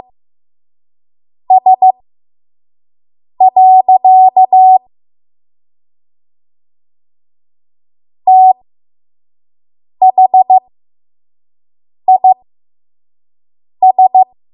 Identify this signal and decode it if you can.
Morse code, heard as a keyed tone, about 15 wpm, spaced as for 5 wpm, 750 Hz. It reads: S. THIS
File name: Morse / 05WPM